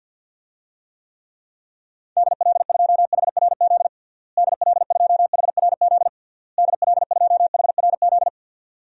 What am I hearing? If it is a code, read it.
Morse code, heard as a keyed tone, about 49 wpm, 700 Hz. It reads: DL2HRZ DL2HRZ DL2HRZ
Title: DXCC-100 Call Signs - 6 Characters - Repeated 3x - 50wpm